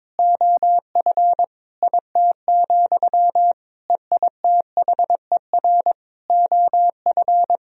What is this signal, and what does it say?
Morse code, heard as a keyed tone, about 22 wpm, 700 Hz. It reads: OF IT, EITHER OF
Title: Sentences from Top 1000 Words - Review Part 4 - 22wpm